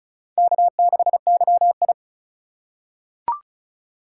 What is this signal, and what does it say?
Morse code, heard as a keyed tone, about 36 wpm, 700 Hz. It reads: K6YI
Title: USA Call Signs - 4x Character Spacing - 35wpm